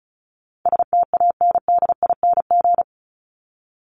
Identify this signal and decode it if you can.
Morse code, heard as a keyed tone, about 34 wpm, 700 Hz. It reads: STANDING
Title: ING Suffix - Top 100 Words with Base Words 35wpm